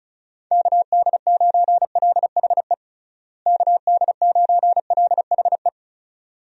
Morse code, heard as a keyed tone, about 36 wpm, 700 Hz. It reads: KD9LHE KD9LHE
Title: USA Call Signs - 6 Characters - Repeated 2x - 35wpm